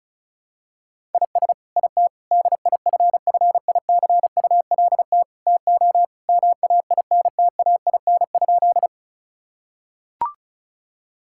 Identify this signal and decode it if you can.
Morse code, heard as a keyed tone, about 34 wpm, 700 Hz. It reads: IS IT DIFFICULT TO MAINTAIN?
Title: Sentences from Top 1000 Words - Review Part 5 - 35wpm